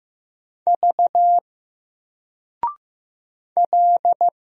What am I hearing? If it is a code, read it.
Morse code, heard as a keyed tone, about 15 wpm, 700 Hz. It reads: V L
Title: Single Letter-Number 15wpm